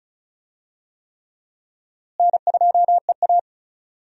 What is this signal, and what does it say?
Morse code, heard as a keyed tone, about 34 wpm, 700 Hz. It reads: N2EA
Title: USA Call Signs - 4 Characters - Repeated 3x - 35wpm